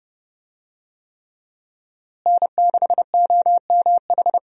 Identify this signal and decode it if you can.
Morse code, heard as a keyed tone, about 30 wpm, 700 Hz. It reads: N6OMH
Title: USA Call Signs - 5 Characters - Repeated 3x - 30wpm